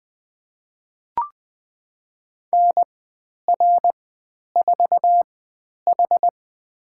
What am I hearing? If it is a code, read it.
Morse code, heard as a keyed tone, about 20 wpm, spaced as for 10 wpm, 700 Hz. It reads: NR4H
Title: 109 Lesson 18g - Letter m - Call Signs 20wpm